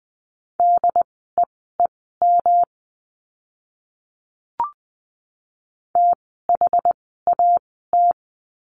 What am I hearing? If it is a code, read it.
Morse code, heard as a keyed tone, about 20 wpm, spaced as for 14 wpm, 700 Hz. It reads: DEEM THAT
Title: 108 Lesson 18f - Letter m - Words 20wpm